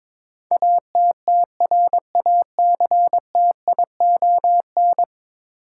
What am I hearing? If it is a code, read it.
Morse code, heard as a keyed tone, about 22 wpm, 700 Hz. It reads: ATTRACTION
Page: morse-code-ninja-practice-22wpm